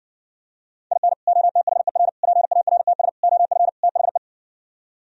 Morse code, heard as a keyed tone, about 59 wpm, 700 Hz. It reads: IN OTHER PARTS OF THE